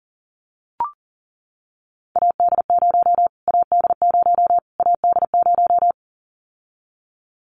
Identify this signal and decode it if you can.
Morse code, heard as a keyed tone, about 40 wpm, 700 Hz. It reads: AD0 AD0 AD0
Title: Callsign Prefixes for Top 25 DXCC - Repeated 3x - 40wpm